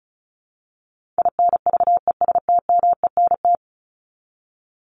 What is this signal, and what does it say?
Morse code, heard as a keyed tone, about 34 wpm, 700 Hz. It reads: INVESTMENT
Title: Top 501-750 Words 35wpm